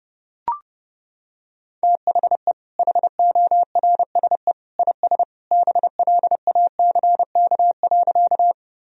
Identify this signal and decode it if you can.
Morse code, heard as a keyed tone, about 30 wpm, 700 Hz. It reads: THE HORSE IS BLACK.
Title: Sentences from Top 1000 Words - Review Part 2 - 30wpm